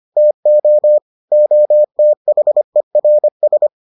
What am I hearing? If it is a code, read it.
Morse code, heard as a keyed tone, about 25 wpm, 600 Hz. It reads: TO OTHERS